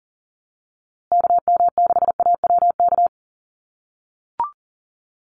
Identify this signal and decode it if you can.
Morse code, heard as a keyed tone, about 40 wpm, 700 Hz. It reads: KM6AWK
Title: USA Call Signs Repeated 2x 40wpm